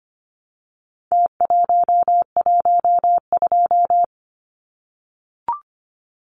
Morse code, heard as a keyed tone, about 25 wpm, 700 Hz. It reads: T112
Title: Numbers - Contest Style - 4-Digit 25wpm